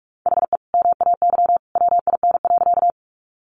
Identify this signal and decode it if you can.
Morse code, heard as a keyed tone, about 44 wpm, 700 Hz. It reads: HE MAY WIN.
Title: Sentences from Top 1500 Words 45wpm